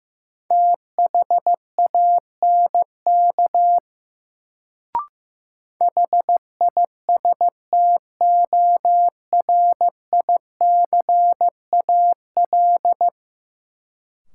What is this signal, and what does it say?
Morse code, heard as a keyed tone, about 15 wpm, 700 Hz. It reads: THANK HISTORICAL
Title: Sentences from Top 600 Words - New Word Drill - 15wpm